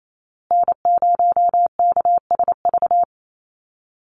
Morse code, heard as a keyed tone, about 28 wpm, 700 Hz. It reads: N0KSV
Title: USA Call Signs Repeated 2x 28wpm